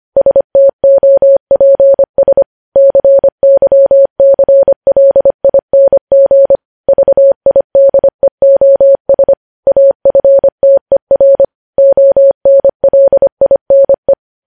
Morse code, heard as a keyed tone, about 25 wpm, 550 Hz. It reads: STOPS CYCLING VIDEOS AFTER ONLINE
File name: podcast / News headlines-1x-25x25